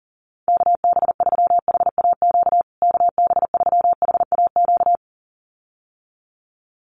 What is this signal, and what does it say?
Morse code, heard as a keyed tone, about 38 wpm, 700 Hz. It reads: KB3HAQ KB3HAQ
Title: USA Call Signs Repeated 2x 40wpm